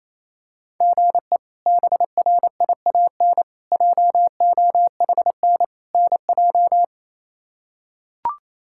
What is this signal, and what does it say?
Morse code, heard as a keyed tone, about 28 wpm, 700 Hz. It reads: GE BRIAN JOHN NJ
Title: K1USN SST QSO Practice - 28wpm